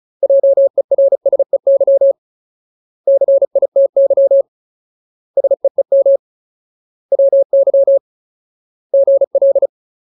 Morse code, heard as a keyed tone, about 34 wpm, 550 Hz. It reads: JERSEY CITY SEEM WY GL
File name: Ultimate Word List / 35WPM Ultimate